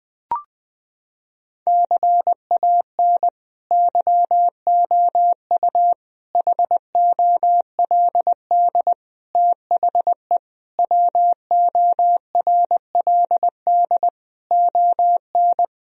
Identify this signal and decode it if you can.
Morse code, heard as a keyed tone, about 20 wpm, 700 Hz. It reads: CAN YOU HOLD THE WORLD ON
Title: Sentences from Top 1000 Words - Review Part 2 - 20wpm